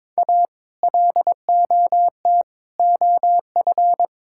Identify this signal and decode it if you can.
Morse code, heard as a keyed tone, about 22 wpm, 700 Hz. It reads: A LOT OF
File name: practice / 22wpm / Sets of 6 Words - 22wpm